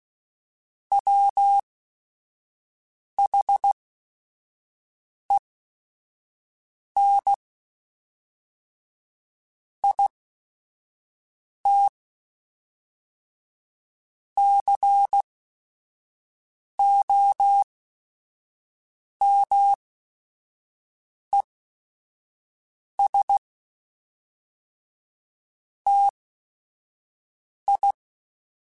Morse code, heard as a keyed tone, about 16 wpm, spaced as for 5 wpm, 800 Hz. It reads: WHEN IT COMES TI
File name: Technical Subjects / Technical Subjects-mp3s / AND Gate @5